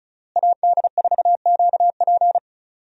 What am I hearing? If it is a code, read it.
Morse code, heard as a keyed tone, about 36 wpm, 700 Hz. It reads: AD4QP
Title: USA Call Signs - 5 Characters - Repeated 3x - 35wpm